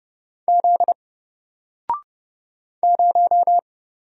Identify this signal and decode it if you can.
Morse code, heard as a keyed tone, about 30 wpm, 700 Hz. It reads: Z 0
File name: Single-Letter-Number-30wpm